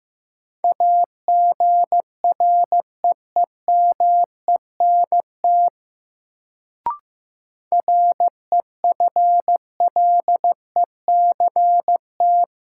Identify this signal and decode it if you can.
Morse code, heard as a keyed tone, about 16 wpm, 700 Hz. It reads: AGREEMENT REFLECT
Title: Sentences from Top 1500 Words - New Word Drill Alt - 15wpm